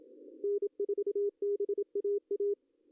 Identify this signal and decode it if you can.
Morse code, heard as a keyed tone, about 25 wpm, 410 Hz. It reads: N4BAA
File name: CW / cqww-cw-k5tr